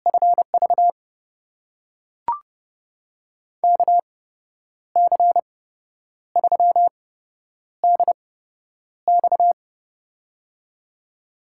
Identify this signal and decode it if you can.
Morse code, heard as a keyed tone, about 30 wpm, spaced as for 10 wpm, 700 Hz. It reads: FV KC3DX
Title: USA Call Signs - 5 Characters - 8x Character Spacing - 30wpm